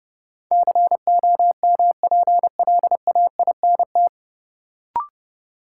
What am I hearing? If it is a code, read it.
Morse code, heard as a keyed tone, about 30 wpm, 700 Hz. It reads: COMPLAINT
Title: Sentences from Top 1400 Words - New Word Drill - 30wpm